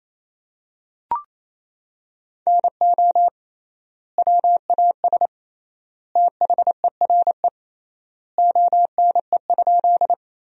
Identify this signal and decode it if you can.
Morse code, heard as a keyed tone, about 28 wpm, 700 Hz. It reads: NO WAS THERE ONE?
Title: 3x Word-Spacing - Sentences from Top 100 Words - 28wpm